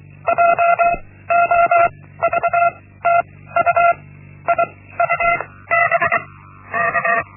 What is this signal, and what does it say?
Morse code, heard as a keyed tone, about 23 wpm, 690 Hz. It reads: J O V T U I U B K
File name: Linksmakalnis-1